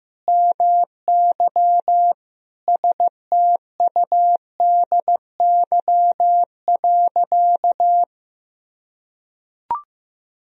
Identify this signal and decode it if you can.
Morse code, heard as a keyed tone, about 15 wpm, 700 Hz. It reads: MY STUDY.
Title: Sentences from Top 1000 Words - Review Part 1 - 15wpm